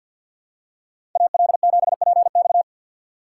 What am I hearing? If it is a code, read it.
Morse code, heard as a keyed tone, about 48 wpm, 700 Hz. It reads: AL7PX
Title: USA Call Signs - 2x Character Spacing - 50wpm